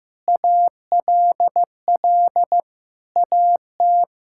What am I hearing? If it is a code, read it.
Morse code, heard as a keyed tone, about 15 wpm, 700 Hz. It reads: ALL AT
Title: Sentences from Top 1000 Words - Review Part 5 - 15wpm